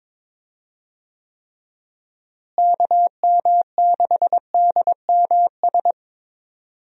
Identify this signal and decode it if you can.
Morse code, heard as a keyed tone, about 21 wpm, 700 Hz. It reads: KM6DMS
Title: USA Call Signs - 6 Characters - Repeated 3x - 22wpm